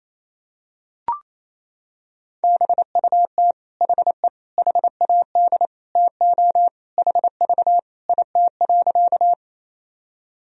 Morse code, heard as a keyed tone, about 28 wpm, 700 Hz. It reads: BUT HE HAD TO HV IT.
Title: Sentences from Top 100 Words and Abbreviations 28wpm